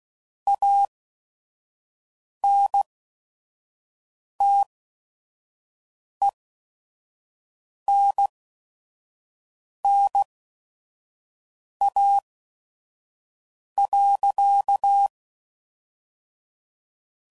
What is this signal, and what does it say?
Morse code, heard as a keyed tone, about 16 wpm, spaced as for 5 wpm, 800 Hz. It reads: ANTENNA.
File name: CW Code Groups @5WPM / QSOs / kc7sac